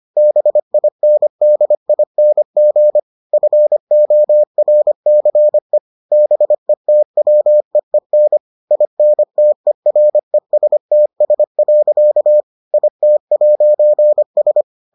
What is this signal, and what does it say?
Morse code, heard as a keyed tone, about 25 wpm, 600 Hz. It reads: BINDING FORCE BETWEEN INTERESTS. IT'S